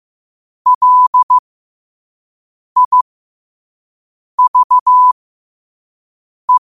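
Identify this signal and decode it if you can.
Morse code, heard as a keyed tone, about 15 wpm, spaced as for 5 wpm, 1000 Hz. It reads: LIVE
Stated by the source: Morse code quote of the day at 5 wpm.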